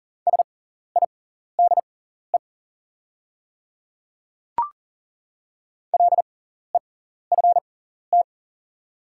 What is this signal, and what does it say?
Morse code, heard as a keyed tone, about 40 wpm, spaced as for 14 wpm, 690 Hz. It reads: SIDE LEFT
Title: 6x Character Spacing - Top 300 Words - 40wpm